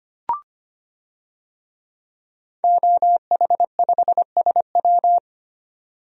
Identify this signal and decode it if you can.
Morse code, heard as a keyed tone, about 25 wpm, 700 Hz. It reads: OH5SW
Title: DXCC-100 Call Signs - 5 Characters - 25wpm